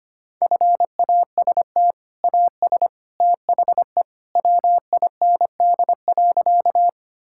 Morse code, heard as a keyed tone, about 25 wpm, 700 Hz. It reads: FAST AS THE WIND.
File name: practice / 25wpm / Sentences with 2-Word Phrases - 25wpm